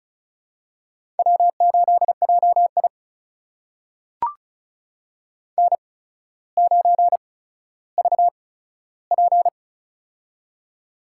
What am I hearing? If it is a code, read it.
Morse code, heard as a keyed tone, about 36 wpm, spaced as for 15 wpm, 700 Hz. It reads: W8JI N9VP
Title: USA Call Signs - 4 Characters - 8x Character Spacing - 35wpm